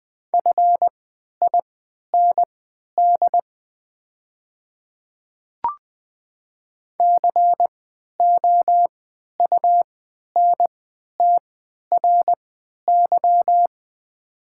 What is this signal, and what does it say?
Morse code, heard as a keyed tone, about 20 wpm, spaced as for 11 wpm, 690 Hz. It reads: FIND COUNTRY
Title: 3x Character Spacing - Top 300 Words - 20wpm